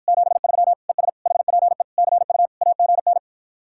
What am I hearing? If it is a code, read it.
Morse code, heard as a keyed tone, about 52 wpm, 700 Hz. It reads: 73 ES HPE CU AGN